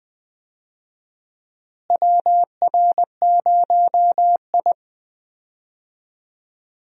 Morse code, heard as a keyed tone, about 20 wpm, 700 Hz. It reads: WR0I
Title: USA Call Signs - 4 Characters - Repeated 3x - 20wpm